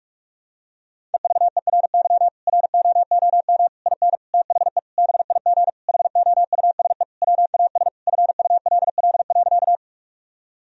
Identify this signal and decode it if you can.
Morse code, heard as a keyed tone, about 46 wpm, 700 Hz. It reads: EVERY ROOM IN THE BIG HOUSE WAS FULL.
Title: Sentences from Top 1000 Words - Review Part 3 - 45wpm